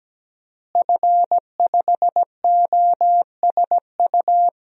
Morse code, heard as a keyed tone, about 17 wpm, 700 Hz. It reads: F5OSU
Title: DXCC-100 Call Signs - 5 Characters - Repeated 2x - 17wpm